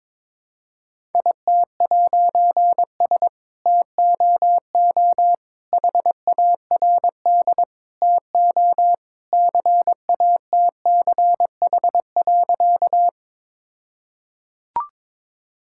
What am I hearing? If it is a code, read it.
Morse code, heard as a keyed tone, about 22 wpm, 690 Hz. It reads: IT'S TOO HARD TO CATCH.
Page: morse-code-ninja-practice-22wpm